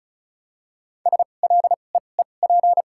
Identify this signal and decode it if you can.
Morse code, heard as a keyed tone, about 36 wpm, spaced as for 25 wpm, 700 Hz. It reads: SLEEP
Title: 2x Character Spacing - Top 500 Words - 35wpm